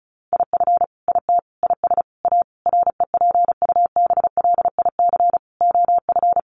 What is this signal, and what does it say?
Morse code, heard as a keyed tone, about 34 wpm, 700 Hz. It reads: IF IT IS A REPUBLIC OF
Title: Sentences from Top 1500 Words 35wpm